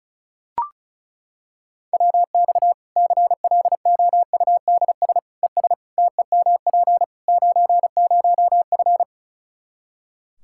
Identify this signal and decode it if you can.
Morse code, heard as a keyed tone, about 36 wpm, 700 Hz. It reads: WX CLOUDS ES TEMP 90F
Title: Practice QSO - Exchange 2 - Info - 35wpm